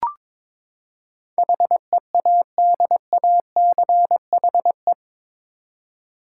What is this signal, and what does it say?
Morse code, heard as a keyed tone, about 21 wpm, 700 Hz. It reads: HEADACHE
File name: practice / 22wpm / Eight Letter Words - 22wpm